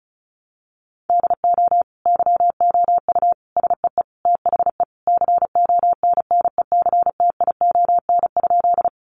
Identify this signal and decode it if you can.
Morse code, heard as a keyed tone, about 34 wpm, 700 Hz. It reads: DO YOU SEE THE CONNECTION?
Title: Sentences from Top 1300 Words 35wpm